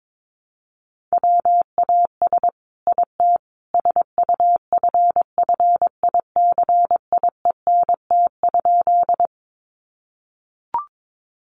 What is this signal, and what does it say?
Morse code, heard as a keyed tone, about 22 wpm, 700 Hz. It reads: WAS IT SUFFICIENT?
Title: Sentences from Top 1400 Words 22wpm